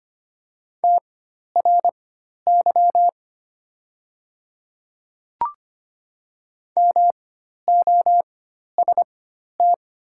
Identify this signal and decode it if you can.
Morse code, heard as a keyed tone, about 25 wpm, spaced as for 12 wpm, 700 Hz. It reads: TRY MOST
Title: 4x Character Spacing - Top 200 Words - 25wpm